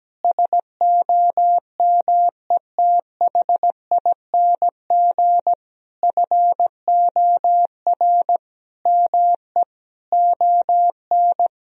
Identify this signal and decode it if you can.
Morse code, heard as a keyed tone, about 17 wpm, 700 Hz. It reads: SOMETHING FOR ME ON
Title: Sentences from Top 1000 Words - Review Part 2 - 17wpm